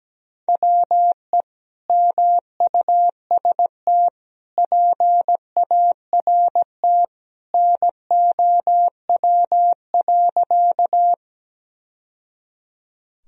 Morse code, heard as a keyed tone, about 17 wpm, 700 Hz. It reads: WE MUST PART NOW.
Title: Sentences from Top 1000 Words - Review Part 1 - 17wpm